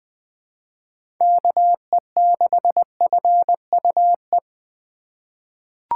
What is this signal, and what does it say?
Morse code, heard as a keyed tone, about 20 wpm, 700 Hz. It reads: KE6FUE
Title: USA Call Signs - 6 Characters - 3x Character Spacing - 20wpm